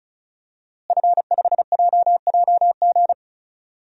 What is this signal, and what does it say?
Morse code, heard as a keyed tone, about 36 wpm, 700 Hz. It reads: F5JJG
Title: DXCC-100 Call Signs - 5 Characters - Repeated 2x - 35wpm